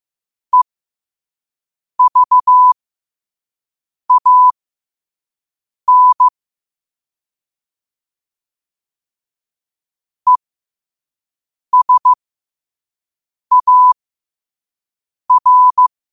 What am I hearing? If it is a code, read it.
Morse code, heard as a keyed tone, about 15 wpm, spaced as for 5 wpm, 1000 Hz. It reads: EVAN ESAR
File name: qotdfiles / 5wpm